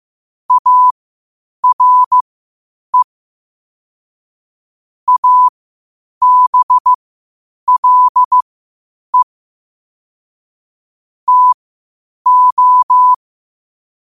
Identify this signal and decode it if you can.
Morse code, heard as a keyed tone, about 15 wpm, spaced as for 8 wpm, 1000 Hz. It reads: ARE ABLE TO
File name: qotdfiles / 8wpm